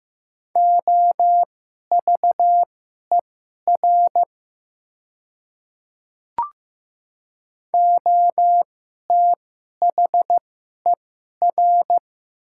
Morse code, heard as a keyed tone, about 15 wpm, spaced as for 11 wpm, 700 Hz. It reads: OVER OTHER
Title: 2x Character Spacing - Top 200 Words - 15wpm